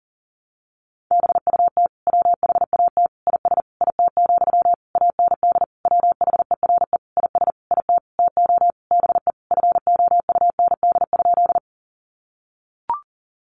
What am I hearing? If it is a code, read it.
Morse code, heard as a keyed tone, about 40 wpm, 700 Hz. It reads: BUT WHAT IS IT, AND WHERE IS IT TO BE FOUND?
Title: Sentences from Top 300 Words 40wpm